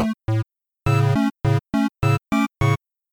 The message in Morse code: none